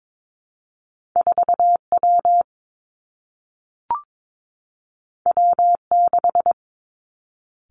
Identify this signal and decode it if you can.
Morse code, heard as a keyed tone, about 22 wpm, 700 Hz. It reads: 4 W W 6
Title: Two Challenging Letter-Number Combinations 22wpm